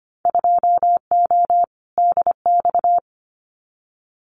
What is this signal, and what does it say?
Morse code, heard as a keyed tone, about 25 wpm, 700 Hz. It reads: 2O DX
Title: Winter Field Day Exchanges Encore 25wpm